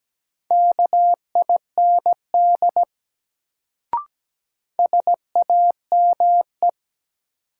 Morse code, heard as a keyed tone, about 17 wpm, 700 Hz. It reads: KIND SAME
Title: Sentences from Top 200 Words - New Word Drill - 17wpm